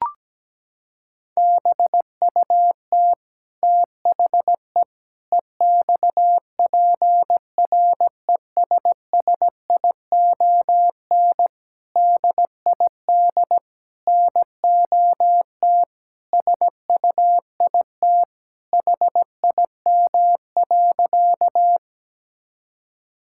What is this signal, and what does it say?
Morse code, heard as a keyed tone, about 17 wpm, 700 Hz. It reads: BUT THE EXPRESSION DID NOT SUIT HIM.
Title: Sentences from Top 1600 Words 17wpm